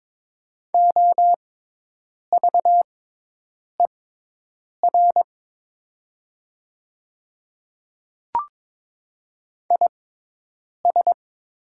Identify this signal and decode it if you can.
Morse code, heard as a keyed tone, about 22 wpm, spaced as for 8 wpm, 700 Hz. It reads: OVER IS
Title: 6x Character Spacing - Top 200 Words - 22wpm